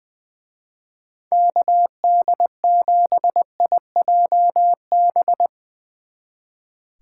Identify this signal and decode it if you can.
Morse code, heard as a keyed tone, about 22 wpm, 700 Hz. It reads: KD7IJB
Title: Speed Racing - USA 2x3 Call Signs 20wpm